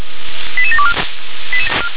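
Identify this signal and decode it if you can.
Five-tone selective call (ZVEI), heard as quick runs of five tones, tones of about 70 ms. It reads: 88724
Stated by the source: ZVEI five-tone call received by SDR.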